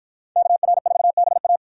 Morse code, heard as a keyed tone, about 51 wpm, 700 Hz. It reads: KR4BA